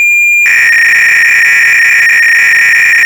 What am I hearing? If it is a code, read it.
300-baud modem data.